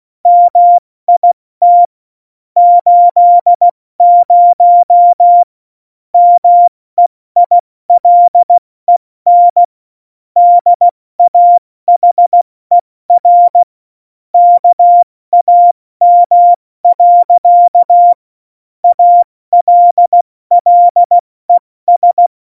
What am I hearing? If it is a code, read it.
Morse code, heard as a keyed tone, about 16 wpm, 700 Hz. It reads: MIT 80 MEILEN DAHER KAM. ALLES
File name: Morsekurs / Witze / Witz 01. Dezember 2020, 70 F80